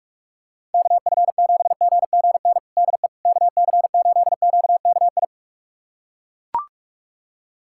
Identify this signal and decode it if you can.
Morse code, heard as a keyed tone, about 41 wpm, 700 Hz. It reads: KF7GGN DE KC8QKI
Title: USA Callsign DE Callsign 45wpm